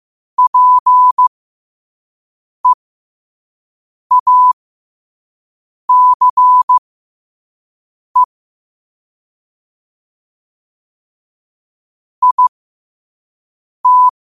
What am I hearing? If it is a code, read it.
Morse code, heard as a keyed tone, about 15 wpm, spaced as for 5 wpm, 1000 Hz. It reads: PEACE IT